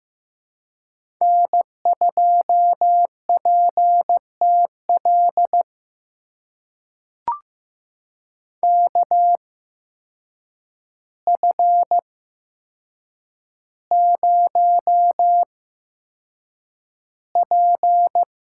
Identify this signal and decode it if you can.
Morse code, heard as a keyed tone, about 15 wpm, spaced as for 7 wpm, 700 Hz. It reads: N2PTL KF0P
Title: USA Call Signs - 5 Characters - 8x Character Spacing - 15wpm